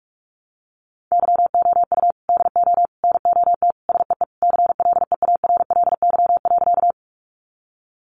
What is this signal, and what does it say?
Morse code, heard as a keyed tone, about 44 wpm, 700 Hz. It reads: YOU DO NOT SEE CLEARLY.
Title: Sentences from Top 1300 Words 45wpm